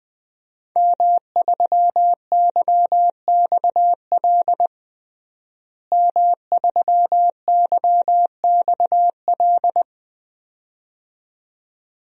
Morse code, heard as a keyed tone, about 20 wpm, 700 Hz. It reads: M3YXL M3YXL
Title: DXCC-100 Call Signs - 5 Characters - Repeated 2x - 20wpm